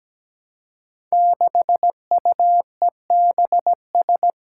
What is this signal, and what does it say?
Morse code, heard as a keyed tone, about 17 wpm, 700 Hz. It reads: 6UEBS
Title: Five Challenging Letter-Number Combinations 17wpm